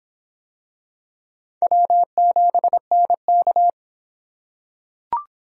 Morse code, heard as a keyed tone, about 26 wpm, 700 Hz. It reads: W7NK